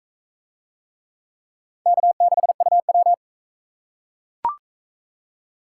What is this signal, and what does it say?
Morse code, heard as a keyed tone, about 40 wpm, 700 Hz. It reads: K6UW